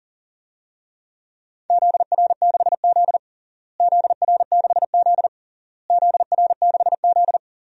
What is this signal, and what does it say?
Morse code, heard as a keyed tone, about 40 wpm, 700 Hz. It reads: ZR6Z ZR6Z ZR6Z
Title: DXCC-100 Call Signs - 4 Characters - Repeated 3x - 40wpm